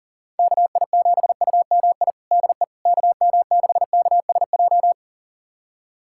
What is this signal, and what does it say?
Morse code, heard as a keyed tone, about 40 wpm, 700 Hz. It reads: KI7UMI DE KM6KSJ
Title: USA Callsign DE Callsign - 6 Characters - 40wpm